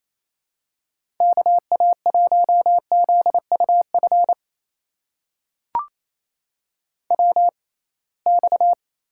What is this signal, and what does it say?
Morse code, heard as a keyed tone, about 28 wpm, spaced as for 21 wpm, 700 Hz. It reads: KA1ZUF WX
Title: USA Call Signs - 6 Characters - 6x Character Spacing - 28wpm